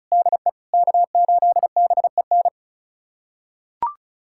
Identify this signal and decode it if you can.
Morse code, heard as a keyed tone, about 36 wpm, 690 Hz. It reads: DE K8BEN
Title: USA Callsign DE Callsign - 5 Characters - 35wpm